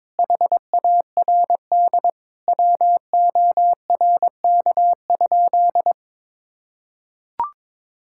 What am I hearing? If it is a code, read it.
Morse code, heard as a keyed tone, about 22 wpm, 700 Hz. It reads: HARD WORK?
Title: Sentences from Top 1600 Words 22wpm